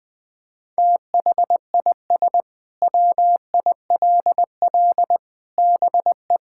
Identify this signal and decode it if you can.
Morse code, heard as a keyed tone, about 20 wpm, 700 Hz. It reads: THIS WILL BE
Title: Sentences with 8-Word Phrases - 20wpm